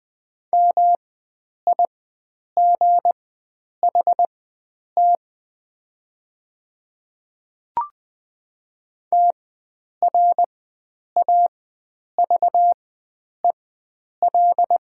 Morse code, heard as a keyed tone, about 20 wpm, spaced as for 9 wpm, 700 Hz. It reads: MIGHT TRAVEL
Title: 4x Character Spacing - Top 500 Words - 20wpm